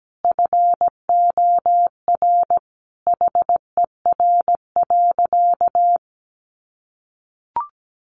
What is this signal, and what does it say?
Morse code, heard as a keyed tone, about 17 wpm, 700 Hz. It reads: FOR HER.
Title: Sentences from Top 900 Words 17wpm